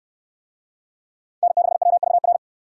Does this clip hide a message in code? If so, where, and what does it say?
Yes: Morse code from 1.4 to 2.4 s, heard as a keyed tone, about 62 wpm, 700 Hz. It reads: N6WVR